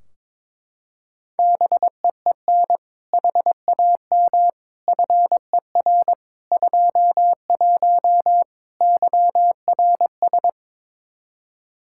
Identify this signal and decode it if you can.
Morse code, heard as a keyed tone, about 21 wpm, 700 Hz. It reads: BEEN HAM FER 21 YRS
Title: Practice QSO - Exchange 3 - More Info - 22wpm